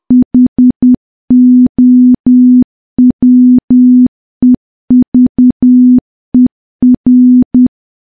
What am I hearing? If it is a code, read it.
Morse code, heard as a keyed tone, about 10 wpm, 250 Hz. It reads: HOWEVER
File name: morse / morse-training-10WPM